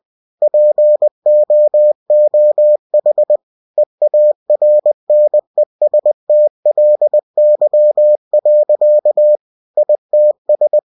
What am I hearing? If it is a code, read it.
Morse code, heard as a keyed tone, about 20 wpm, 600 Hz. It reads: POOH EARNESTLY. ITS
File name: cw-books / winnie-the-pooh / winnie-20wpm-page-0051